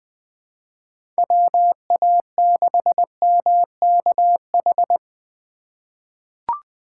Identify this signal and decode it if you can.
Morse code, heard as a keyed tone, about 20 wpm, 700 Hz. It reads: WA6MKH
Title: USA Call Signs - 4x Character Spacing - 20wpm